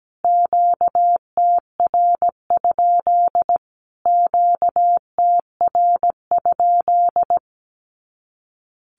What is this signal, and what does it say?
Morse code, heard as a keyed tone, about 17 wpm, 700 Hz. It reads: QTR? QTR?
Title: Q-Codes Repeated 3x 17wpm